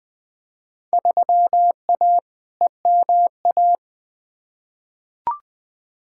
Morse code, heard as a keyed tone, about 20 wpm, 700 Hz. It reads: 3A EMA
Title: Field Day Exchanges - 20wpm